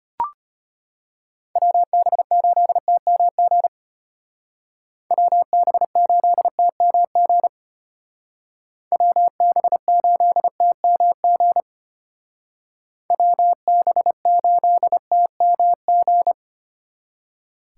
Morse code, heard as a keyed tone, about 31 wpm, 700 Hz. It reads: WB8TMG WB8TMG WB8TMG WB8TMG
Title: Speed Racing - USA 2x3 Call Signs 25wpm